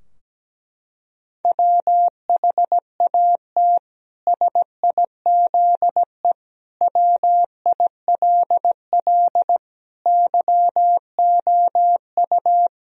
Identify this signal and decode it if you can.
Morse code, heard as a keyed tone, about 17 wpm, 700 Hz. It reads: WHAT SIZE WILL YOU
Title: Sentences from Top 1000 Words - Review Part 2 - 17wpm